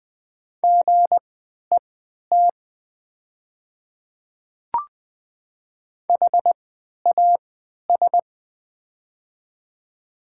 Morse code, heard as a keyed tone, about 20 wpm, spaced as for 11 wpm, 700 Hz. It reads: GET HAS
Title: 3x Character Spacing - Top 200 Words - 20wpm